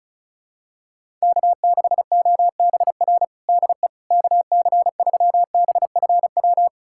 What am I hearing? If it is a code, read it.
Morse code, heard as a keyed tone, about 34 wpm, 700 Hz. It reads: K6OBR DE KC3BFW
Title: USA Callsign DE Callsign 35wpm